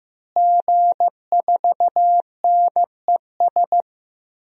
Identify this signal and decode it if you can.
Morse code, heard as a keyed tone, about 15 wpm, 700 Hz. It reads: G4NES
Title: DXCC-100 Call Signs - 5 Characters - Repeated 2x - 15wpm